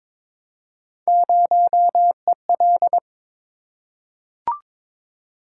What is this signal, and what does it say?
Morse code, heard as a keyed tone, about 22 wpm, 700 Hz. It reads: 0EL
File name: practice / 22wpm / Three Letter-Number Combinations - 22wpm